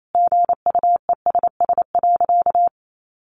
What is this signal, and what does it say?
Morse code, heard as a keyed tone, about 28 wpm, 700 Hz. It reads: GUESS.
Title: Sentences from Top 1500 Words 28wpm